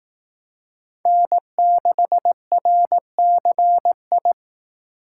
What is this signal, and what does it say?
Morse code, heard as a keyed tone, about 18 wpm, 700 Hz. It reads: N6RCI
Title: Speed Racing - USA 1x3 Call Signs 15wpm